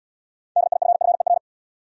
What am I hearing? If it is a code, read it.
Morse code, heard as a keyed tone, about 73 wpm, 700 Hz. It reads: DIFFER